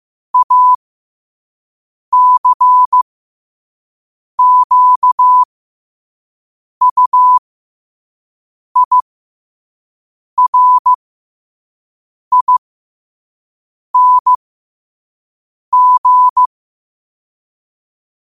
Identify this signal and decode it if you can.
Morse code, heard as a keyed tone, about 15 wpm, spaced as for 5 wpm, 1000 Hz. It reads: ACQUIRING